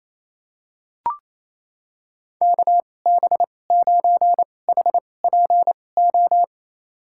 Morse code, heard as a keyed tone, about 28 wpm, spaced as for 20 wpm, 700 Hz. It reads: KB9HPO
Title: USA Call Signs - 6 Characters - 2x Character Spacing - 28wpm